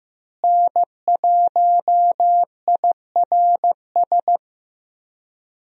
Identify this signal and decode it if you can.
Morse code, heard as a keyed tone, about 15 wpm, 700 Hz. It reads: N1IRS
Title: USA Call Signs - 5 Characters - Repeated 3x - 15wpm